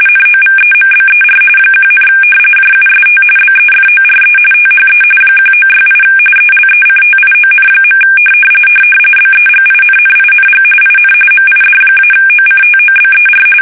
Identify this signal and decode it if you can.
RTTY, about 45.45 baud. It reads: NOVEMBER 1, JUST BEFORE THE 1968 ELECTION. BEGINNING IN MID-OCTOBER
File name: fsk850 / battleshipnj